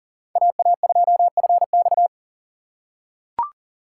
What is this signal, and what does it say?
Morse code, heard as a keyed tone, about 40 wpm, 700 Hz. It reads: AA2FX
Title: USA Call Signs - 5 Characters - 3x Character Spacing - 40wpm